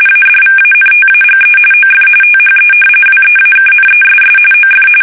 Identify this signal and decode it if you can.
RTTY, about 45.45 baud. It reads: ADVANTAGE OF THE WEAKNESS OF
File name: fsk850 / usnavy2